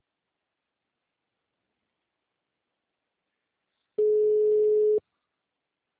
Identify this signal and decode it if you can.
Ringback tone (Europe, 425 Hz).
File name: out- -2013- - - .21